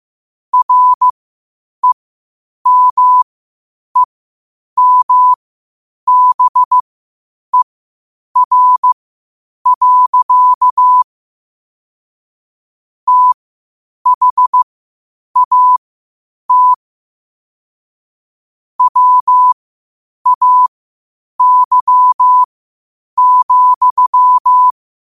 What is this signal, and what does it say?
Morse code, heard as a keyed tone, about 15 wpm, spaced as for 8 wpm, 1000 Hz. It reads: REMEMBER. THAT WAY,
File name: qotdfiles / 8wpm